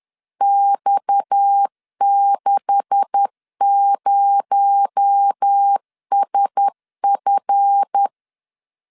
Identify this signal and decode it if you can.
Morse code, heard as a keyed tone, about 10 wpm, 800 Hz. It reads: X60SF
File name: CW / DET / 1 9 1500C S B 10wpm 4439